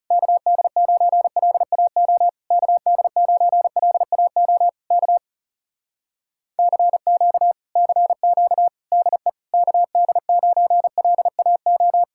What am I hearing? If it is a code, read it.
Morse code, heard as a keyed tone, about 40 wpm, 700 Hz. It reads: KD9LAO KD9LAO K CQ CQ DE KD9LAO